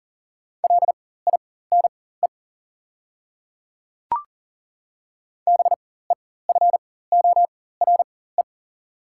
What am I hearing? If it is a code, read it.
Morse code, heard as a keyed tone, about 40 wpm, spaced as for 19 wpm, 690 Hz. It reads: LINE BEFORE
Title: 4x Character Spacing - Top 200 Words - 40wpm